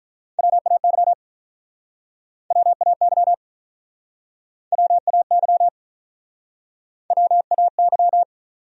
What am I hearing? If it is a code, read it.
Morse code, heard as a keyed tone, about 46 wpm, 700 Hz. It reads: WAY WAY WAY WAY